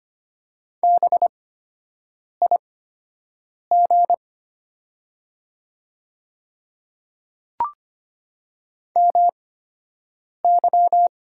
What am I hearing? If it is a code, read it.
Morse code, heard as a keyed tone, about 25 wpm, spaced as for 7 wpm, 700 Hz. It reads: BIG MY
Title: 8x Character Spacing - Top 300 Words - 25wpm